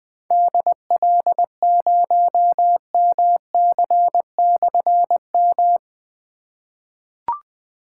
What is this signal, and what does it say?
Morse code, heard as a keyed tone, about 21 wpm, 700 Hz. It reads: DL0MC/M
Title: Difficult Call Signs 20wpm